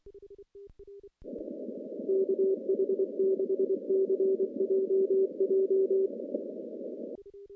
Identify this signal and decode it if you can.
Morse code, heard as a keyed tone, about 23 wpm, 380 Hz. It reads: KH6CJJ
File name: contests / CW-SS-k5tr- -0400